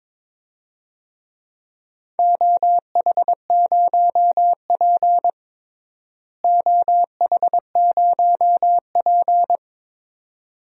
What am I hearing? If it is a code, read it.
Morse code, heard as a keyed tone, about 22 wpm, 700 Hz. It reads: OH0P OH0P
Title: DXCC-100 Call Signs - 4 Characters - Repeated 2x - 22wpm